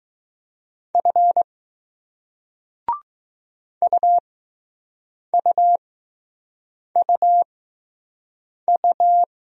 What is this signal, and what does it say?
Morse code, heard as a keyed tone, about 20 wpm, spaced as for 6 wpm, 700 Hz. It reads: F UUUU